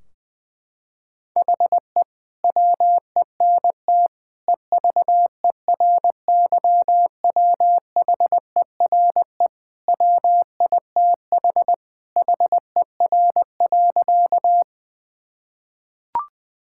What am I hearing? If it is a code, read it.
Morse code, heard as a keyed tone, about 21 wpm, 700 Hz. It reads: HE WENT EVERYWHERE WITH HER.
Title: Sentences from Top 1600 Words 20wpm